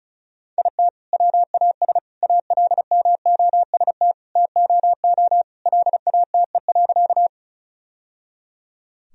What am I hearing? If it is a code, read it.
Morse code, heard as a keyed tone, about 36 wpm, 700 Hz. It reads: IT WAS ALMOST TOO LATE.
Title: Sentences from Top 1000 Words - Review Part 4 - 35wpm